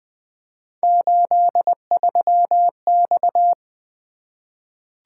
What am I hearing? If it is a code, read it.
Morse code, heard as a keyed tone, about 20 wpm, 700 Hz. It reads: 83X
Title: Three Letter-Number Combinations - 20wpm